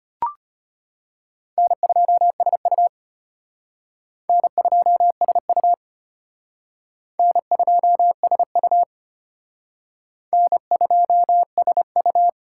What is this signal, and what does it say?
Morse code, heard as a keyed tone, about 31 wpm, 700 Hz. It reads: N2SU N2SU N2SU N2SU
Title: Speed Racing - USA 1x2 Call Signs 25wpm